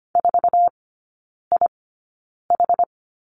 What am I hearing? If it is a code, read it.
Morse code, heard as a keyed tone, about 25 wpm, spaced as for 9 wpm, 700 Hz. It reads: 4IH
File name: course / 25:10wpm / 093 Lesson 16e - Letter c - Three Characters 25wpm